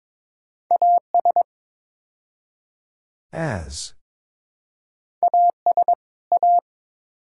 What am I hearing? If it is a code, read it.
Morse code, heard as a keyed tone, about 22 wpm, 700 Hz. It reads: AS AS A
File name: practice / 22wpm / Build A Phrase - 5-Word Phrases - 22wpm